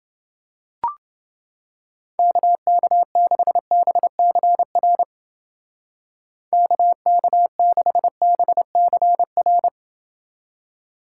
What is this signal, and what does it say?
Morse code, heard as a keyed tone, about 29 wpm, 700 Hz. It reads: KK6BCR KK6BCR